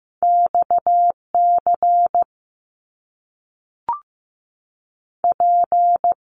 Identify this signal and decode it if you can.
Morse code, heard as a keyed tone, about 15 wpm, 700 Hz. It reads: X C P
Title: Top Two Letter Combinations in Words 15wpm